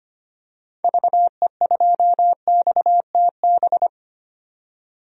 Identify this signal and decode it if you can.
Morse code, heard as a keyed tone, about 25 wpm, 700 Hz. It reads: VE2XTB
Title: DXCC-100 Call Signs - 6 Characters - 25wpm